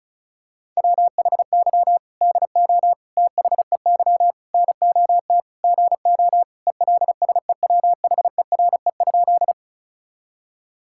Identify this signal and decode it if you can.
Morse code, heard as a keyed tone, about 36 wpm, 700 Hz. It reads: WHY DO THEY NOT GO ELSEWHERE?
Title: Sentences from Top 1600 Words 35wpm